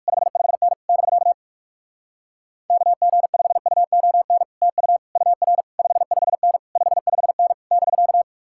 Morse code, heard as a keyed tone, about 59 wpm, 700 Hz. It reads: 55N <BK> KG5UON TU UR 55N 55N <BK>